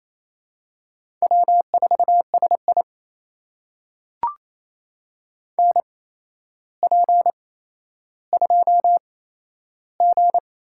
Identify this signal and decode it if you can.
Morse code, heard as a keyed tone, about 28 wpm, spaced as for 12 wpm, 700 Hz. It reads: W4SI NP2G
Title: USA Call Signs - 4 Characters - 8x Character Spacing - 28wpm